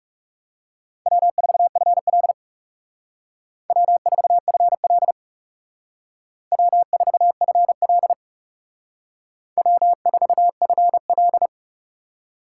Morse code, heard as a keyed tone, about 39 wpm, 700 Hz. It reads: W4FL W4FL W4FL W4FL